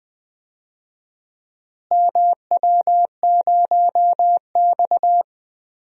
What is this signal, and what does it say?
Morse code, heard as a keyed tone, about 21 wpm, 700 Hz. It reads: MW0X
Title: DXCC-100 Call Signs - 4 Characters - Repeated 2x - 20wpm